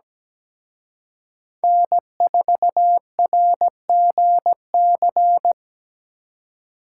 Morse code, heard as a keyed tone, about 20 wpm, 700 Hz. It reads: N4RGC
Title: Speed Racing - USA 1x3 Call Signs 17wpm